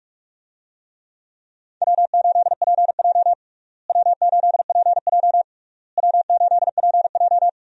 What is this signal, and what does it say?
Morse code, heard as a keyed tone, about 46 wpm, 700 Hz. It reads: W8PJ W8PJ W8PJ
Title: USA Call Signs - 4 Characters - Repeated 3x - 45wpm